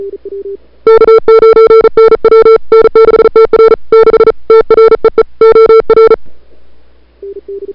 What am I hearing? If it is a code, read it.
Morse code, heard as a keyed tone, about 39 wpm, 440 Hz. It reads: K9NW N6TR 6 TREE OR
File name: CWSprint / February 2007 CW Sprint / 01-2007 CW Sprint Hour1